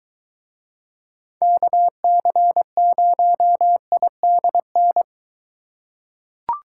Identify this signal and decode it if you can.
Morse code, heard as a keyed tone, about 23 wpm, 700 Hz. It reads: KC0IDN